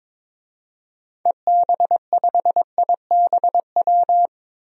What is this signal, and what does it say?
Morse code, heard as a keyed tone, about 22 wpm, 700 Hz. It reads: EB5IBW
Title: DXCC-100 Call Signs - 6 Characters - 22wpm